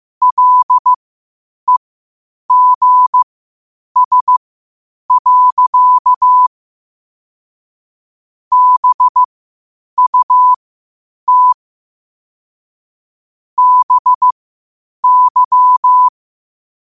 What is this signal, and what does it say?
Morse code, heard as a keyed tone, about 15 wpm, spaced as for 8 wpm, 1000 Hz. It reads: LEGS. BUT BY
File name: qotdfiles / 8wpm